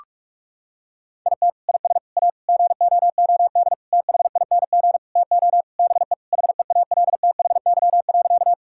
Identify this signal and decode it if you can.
Morse code, heard as a keyed tone, about 41 wpm, 700 Hz. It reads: IT IS A GOOD THING TO BE HEALTHY.
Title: Sentences from Top 1000 Words - Review Part 5 - 45wpm